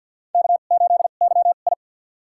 Morse code, heard as a keyed tone, about 51 wpm, spaced as for 36 wpm, 700 Hz. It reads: K8YI
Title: USA Call Signs - 4 Characters - 2x Character Spacing - 50wpm